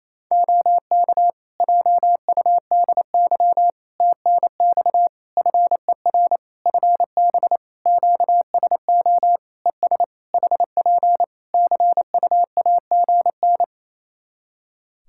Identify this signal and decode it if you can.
Morse code, heard as a keyed tone, about 28 wpm, 700 Hz. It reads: OK JUDY TNX FER FB QSO ES HP CUAGN
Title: Practice QSO - Exchange 4 - Ending - 28wpm